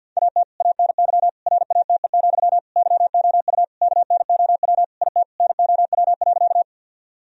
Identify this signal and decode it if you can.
Morse code, heard as a keyed tone, about 50 wpm, 700 Hz. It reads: AT ANY RATE, YOU KNOW IT NOW.
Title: Sentences from Top 1000 Words - Review Part 4 - 50wpm